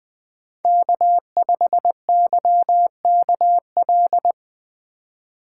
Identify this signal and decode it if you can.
Morse code, heard as a keyed tone, about 20 wpm, 700 Hz. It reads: K5YKL
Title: USA Call Signs - 5 Characters - 3x Character Spacing - 20wpm